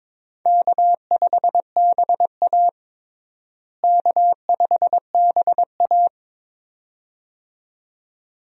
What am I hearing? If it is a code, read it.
Morse code, heard as a keyed tone, about 22 wpm, 700 Hz. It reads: K5BA K5BA
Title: USA Call Signs - 4 Characters - Repeated 2x - 22wpm